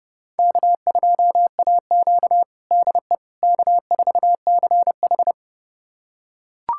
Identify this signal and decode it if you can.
Morse code, heard as a keyed tone, about 30 wpm, 700 Hz. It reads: K2AQ DE K4CH
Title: USA Callsign DE Callsign - 4 Characters - 30wpm